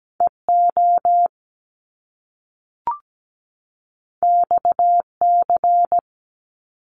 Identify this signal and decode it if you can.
Morse code, heard as a keyed tone, about 17 wpm, 700 Hz. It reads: EO XC
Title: Top Two Letter Combinations in Words 17wpm